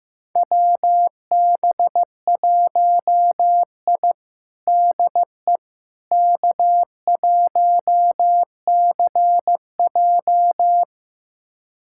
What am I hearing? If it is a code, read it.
Morse code, heard as a keyed tone, about 15 wpm, 700 Hz. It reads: WB1I DE K1CJ
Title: USA Callsign DE Callsign - 4 Characters - 15wpm